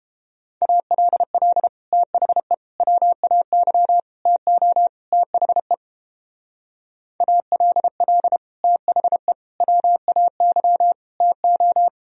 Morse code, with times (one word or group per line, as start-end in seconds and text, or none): 0.6-1.7 ALL
1.9-2.5 THE
2.8-4.0 WAY
4.3-4.9 TO
5.1-5.7 THE
7.2-8.4 ALL
8.6-9.3 THE
9.6-10.9 WAY
11.2-11.9 TO